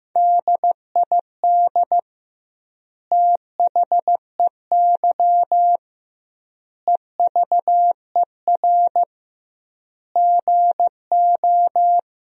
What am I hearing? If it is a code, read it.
Morse code, heard as a keyed tone, about 16 wpm, 700 Hz. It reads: DID THEY EVER GO
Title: 2x Word-Spacing - Sentences from Top 400 Words - 15wpm